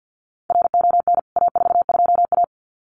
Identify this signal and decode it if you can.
Morse code, heard as a keyed tone, about 50 wpm, 700 Hz. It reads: RON A42A
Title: CWT Exchange 50wpm